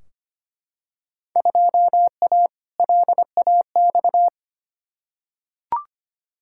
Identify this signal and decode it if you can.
Morse code, heard as a keyed tone, about 25 wpm, 700 Hz. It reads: 2A LAX
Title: Field Day Exchanges - 25wpm